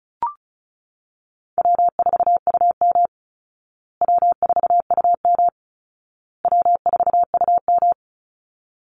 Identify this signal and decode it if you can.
Morse code, heard as a keyed tone, about 34 wpm, 700 Hz. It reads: W4UM W4UM W4UM
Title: Easy Call Signs Repeated 3x - 35wpm